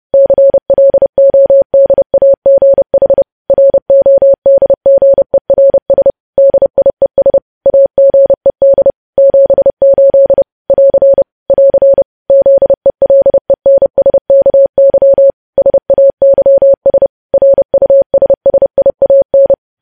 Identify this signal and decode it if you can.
Morse code, heard as a keyed tone, about 30 wpm, 550 Hz. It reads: CLODAGH RODGERS DIES AGED 78 <AR> <AR> ZELENSKY SAYS RUSSIAN
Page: news headlines in morse code at 30 wpm